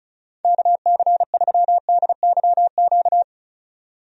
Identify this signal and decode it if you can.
Morse code, heard as a keyed tone, about 36 wpm, 700 Hz. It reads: KC3DYQ
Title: USA Call Signs - 6 Characters - 35wpm